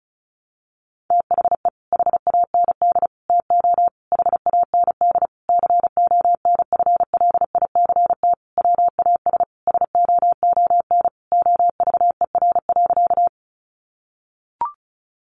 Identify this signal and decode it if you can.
Morse code, heard as a keyed tone, about 34 wpm, 700 Hz. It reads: THE HAND TO HAND CONFLICT WAS SOON OVER.
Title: Sentences from Top 1300 Words 35wpm